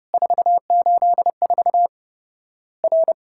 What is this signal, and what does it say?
Morse code, heard as a keyed tone, about 30 wpm, 700 Hz. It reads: 484 R
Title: Worked All Europe DX Contest - QTC Exchange - 30wpm